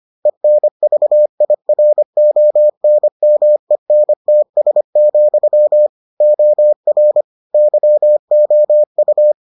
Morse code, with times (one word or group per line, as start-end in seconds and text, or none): 0.3-5.9 ENVIRONMENTS,
6.2-7.2 OR
7.5-9.3 YOU